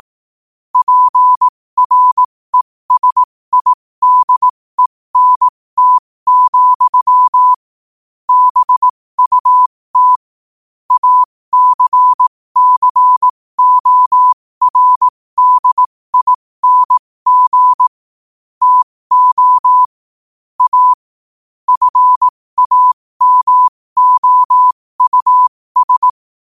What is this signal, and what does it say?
Morse code, heard as a keyed tone, about 18 wpm, spaced as for 15 wpm, 1000 Hz. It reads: PRESIDENT, BUT ACCORDING TO A FAMOUS
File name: qotdfiles / 15wpm